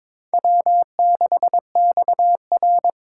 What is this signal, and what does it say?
Morse code, heard as a keyed tone, about 21 wpm, 700 Hz. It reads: W6XR
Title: K1USN SST QSO Practice - 22wpm